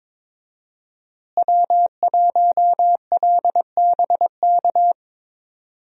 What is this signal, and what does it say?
Morse code, heard as a keyed tone, about 22 wpm, 700 Hz. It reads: W1LBK
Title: USA Call Signs - 5 Characters - Repeated 3x - 22wpm